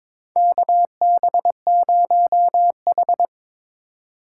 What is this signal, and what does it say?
Morse code, heard as a keyed tone, about 22 wpm, 700 Hz. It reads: KB0H
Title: USA Call Signs - 4 Characters - 4x Character Spacing - 22wpm